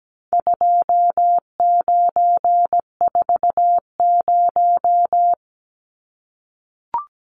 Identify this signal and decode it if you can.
Morse code, heard as a keyed tone, about 17 wpm, 700 Hz. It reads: 2940
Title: Numbers - 4-Digit 17wpm